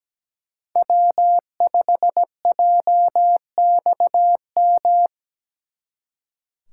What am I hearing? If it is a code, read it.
Morse code, heard as a keyed tone, about 20 wpm, 700 Hz. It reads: W5JXM
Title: Speed Racing - USA 1x3 Call Signs 17wpm